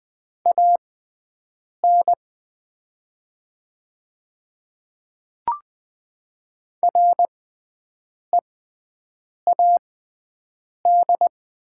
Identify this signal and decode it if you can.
Morse code, heard as a keyed tone, about 20 wpm, spaced as for 7 wpm, 700 Hz. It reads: AN READ
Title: 6x Character Spacing - Top 500 Words - 20wpm